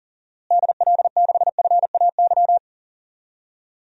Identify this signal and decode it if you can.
Morse code, heard as a keyed tone, about 40 wpm, 700 Hz. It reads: DL6FAY
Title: DXCC-100 Call Signs - 6 Characters - Repeated 3x - 40wpm